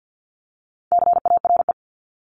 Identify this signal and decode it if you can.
Morse code, heard as a keyed tone, about 50 wpm, 700 Hz. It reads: CARE
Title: Top 500 Words 50wpm